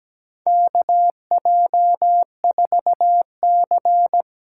Morse code, heard as a keyed tone, about 17 wpm, 700 Hz. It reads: KJ4C
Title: USA Call Signs - 6 Characters - 8x Character Spacing - 17wpm